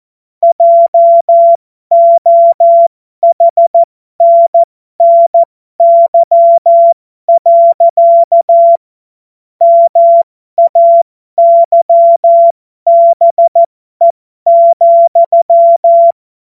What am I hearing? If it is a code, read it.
Morse code, heard as a keyed tone, about 14 wpm, spaced as for 12 wpm, 680 Hz. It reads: JOHNNY. MAYBE,
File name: Morsekurs / Witze